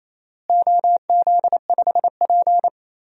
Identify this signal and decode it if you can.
Morse code, heard as a keyed tone, about 27 wpm, 700 Hz. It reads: OZ5P
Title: DXCC-100 Call Signs - 4 Characters - Repeated 2x - 28wpm